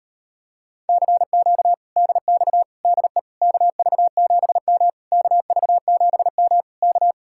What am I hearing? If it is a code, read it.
Morse code, heard as a keyed tone, about 38 wpm, 700 Hz. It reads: CQ DX DE KV7M KV7M K